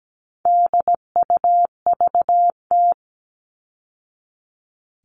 Morse code, heard as a keyed tone, about 17 wpm, 700 Hz. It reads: DUVT
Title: All Cut Numbers - 4-Digit 17wpm